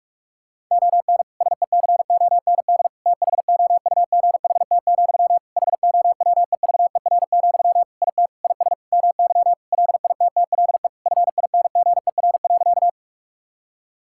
Morse code, heard as a keyed tone, about 41 wpm, 700 Hz. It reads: ON SECOND THOUGHT, HOWEVER, IT IS MY LITTLE FINGER.
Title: Sentences from Top 1600 Words 45wpm